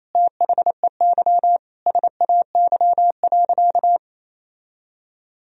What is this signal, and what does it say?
Morse code, heard as a keyed tone, about 28 wpm, 700 Hz. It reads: THEY SAY.
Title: Sentences from Top 1000 Words - Review Part 1 - 28wpm